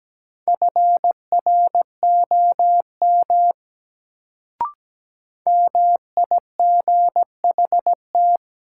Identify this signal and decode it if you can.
Morse code, heard as a keyed tone, about 17 wpm, 700 Hz. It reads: FROM MIGHT
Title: Sets of 2 Words - New Word Drill - 17wpm